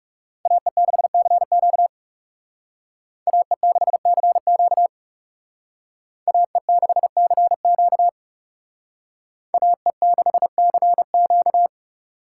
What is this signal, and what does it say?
Morse code, heard as a keyed tone, about 38 wpm, 700 Hz. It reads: AE6CQ AE6CQ AE6CQ AE6CQ